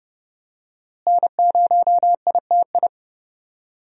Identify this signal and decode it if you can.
Morse code, heard as a keyed tone, about 30 wpm, 700 Hz. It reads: N0ITI
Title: USA Call Signs - 5 Characters - Repeated 2x - 30wpm